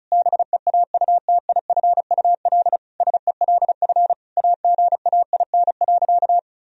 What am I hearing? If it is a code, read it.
Morse code, heard as a keyed tone, about 36 wpm, 700 Hz. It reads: BEAUTIFUL SELF AGAIN.
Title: Sentences from Top 1000 Words - Review Part 1 - 35wpm